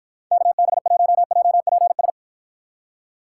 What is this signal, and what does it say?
Morse code, heard as a keyed tone, about 52 wpm, 700 Hz. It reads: KB1JPS